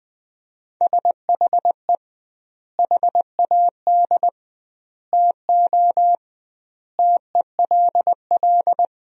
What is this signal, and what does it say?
Morse code, heard as a keyed tone, about 20 wpm, 700 Hz. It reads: SHE HAD TO TELL
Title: 2x Word-Spacing - Sentences from Top 200 Words - 20wpm